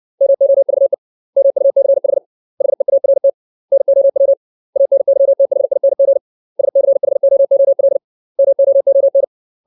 Morse code, heard as a keyed tone, about 60 wpm, 550 Hz. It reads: MOVE MUCH HEART NOW ANYTHING SCHOOL MOON
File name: 632 words / 632 Words 60WPM